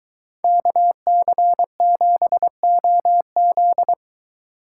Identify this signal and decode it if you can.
Morse code, heard as a keyed tone, about 23 wpm, 700 Hz. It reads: KC7OZ